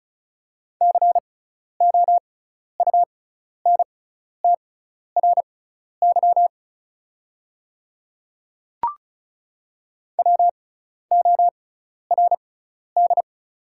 Morse code, heard as a keyed tone, about 36 wpm, spaced as for 12 wpm, 700 Hz. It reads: COUNTRY WORD
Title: 6x Character Spacing - Top 300 Words - 35wpm